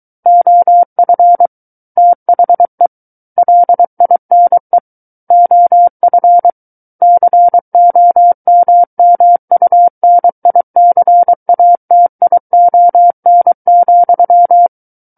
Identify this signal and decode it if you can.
Morse code, heard as a keyed tone, about 23 wpm, 700 Hz. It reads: OF THE LINE OF COMMUNICATION,
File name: Book One / 14-Chapter 14 - 23 WPM